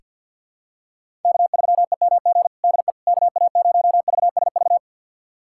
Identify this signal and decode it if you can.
Morse code, heard as a keyed tone, about 47 wpm, 700 Hz. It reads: K3EMG DE KA0VSV
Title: USA Callsign DE Callsign 50wpm